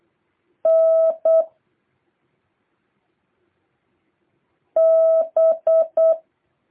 Morse code, heard as a keyed tone, about 8 wpm, 650 Hz. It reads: N B
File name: jh8cbh / 2023cw